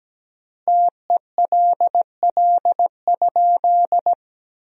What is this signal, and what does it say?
Morse code, heard as a keyed tone, about 17 wpm, 700 Hz. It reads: TELL?
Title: 4x Word-Spacing - Sentences from Top 200 Words - 17wpm